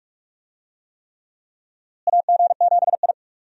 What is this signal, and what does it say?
Morse code, heard as a keyed tone, about 43 wpm, 700 Hz. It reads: AG7I
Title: USA Call Signs - 4 Characters - Repeated 3x - 45wpm